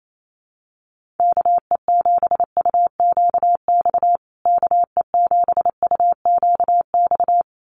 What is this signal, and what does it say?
Morse code, heard as a keyed tone, about 28 wpm, 700 Hz. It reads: KE7UQX KE7UQX
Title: USA Call Signs Repeated 2x 28wpm